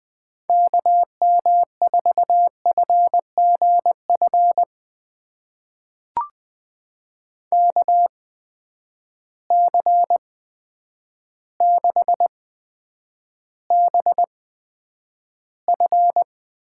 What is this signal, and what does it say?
Morse code, heard as a keyed tone, about 20 wpm, spaced as for 9 wpm, 700 Hz. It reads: KM4FGF KC6BF
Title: USA Call Signs - 6 Characters - 8x Character Spacing - 20wpm